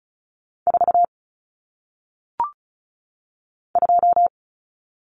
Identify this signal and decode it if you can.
Morse code, heard as a keyed tone, about 35 wpm, 700 Hz. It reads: 4 2
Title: Common Cut Numbers - 1-Digit 35wpm